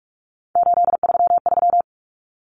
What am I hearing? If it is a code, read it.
Morse code, heard as a keyed tone, about 44 wpm, 700 Hz. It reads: 833
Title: Numbers - 3-Digit 45wpm